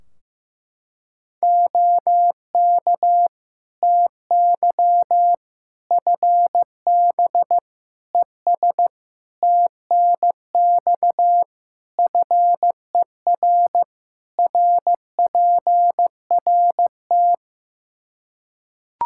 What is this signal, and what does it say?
Morse code, heard as a keyed tone, about 15 wpm, 700 Hz. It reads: OK TY FB ES TNX FER RPRT
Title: Practice QSO - Exchange 2 - Info - 15wpm